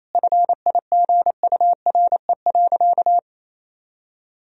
Morse code, heard as a keyed tone, about 28 wpm, 700 Hz. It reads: FIGURE.
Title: Sentences from Top 1000 Words - Review Part 2 - 28wpm